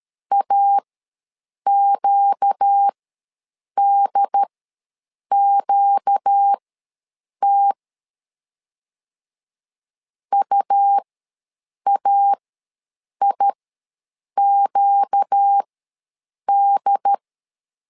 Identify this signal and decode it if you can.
Morse code, heard as a keyed tone, about 13 wpm, spaced as for 7 wpm, 800 Hz. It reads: AQDQT UAIQD
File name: LEZIONI / 4 Q ws 12wpm 1912